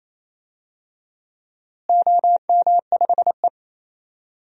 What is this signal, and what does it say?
Morse code, heard as a keyed tone, about 27 wpm, 700 Hz. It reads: OM5E
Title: DXCC-100 Call Signs - 4 Characters - Repeated 2x - 28wpm